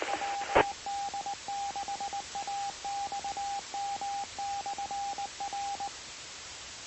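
Morse code, heard as a keyed tone, about 19 wpm, 800 Hz, with radio noise. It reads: WD6AXM/R
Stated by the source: Morse signal received on 2 m at 146.085 MHz.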